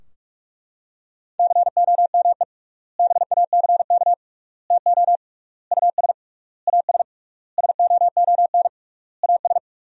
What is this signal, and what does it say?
Morse code, heard as a keyed tone, about 41 wpm, 700 Hz. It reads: COME BACK TO US AS SOON AS
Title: 3x Word-Spacing - Sentences from Top 500 Words - 45wpm